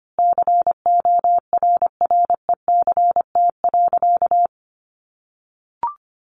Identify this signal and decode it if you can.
Morse code, heard as a keyed tone, about 25 wpm, 700 Hz. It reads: CORRECT.
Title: A Sound Of Thunder 25wpm